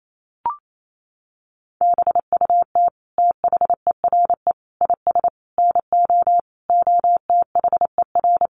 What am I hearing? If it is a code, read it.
Morse code, heard as a keyed tone, about 28 wpm, 700 Hz. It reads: BUT THERE IS NO OTHER
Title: Sentences from Top 1100 Words 28wpm